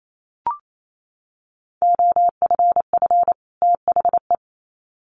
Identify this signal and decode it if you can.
Morse code, heard as a keyed tone, about 28 wpm, 700 Hz. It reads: OFF THE
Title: Sets of 2 Words 28wpm